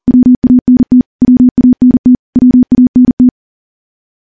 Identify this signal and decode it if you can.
Morse code, heard as a keyed tone, about 40 wpm, 250 Hz. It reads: WANT WANT WANT
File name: morse / morse-training-40WPM